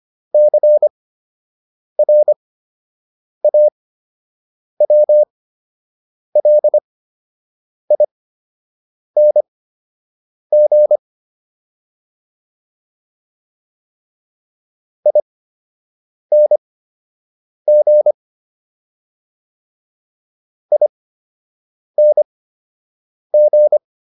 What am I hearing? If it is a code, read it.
Morse code, heard as a keyed tone, about 25 wpm, spaced as for 7 wpm, 600 Hz. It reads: CRAWLING ING ING